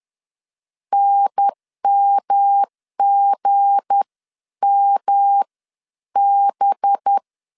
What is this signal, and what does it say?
Morse code, heard as a keyed tone, about 10 wpm, spaced as for 9 wpm, 800 Hz. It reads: NMGMB
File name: CW / DET / 1 9 1500C S A 10wpm 4511